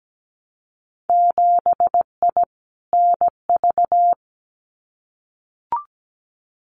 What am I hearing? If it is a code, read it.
Morse code, heard as a keyed tone, about 17 wpm, 700 Hz. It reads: 7I NV
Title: Winter Field Day Exchanges Encore 17wpm